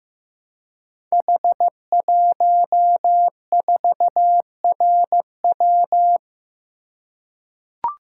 Morse code, heard as a keyed tone, about 15 wpm, 700 Hz. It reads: H14RW
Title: Code Groups - 5 Letter:Numbers - 15wpm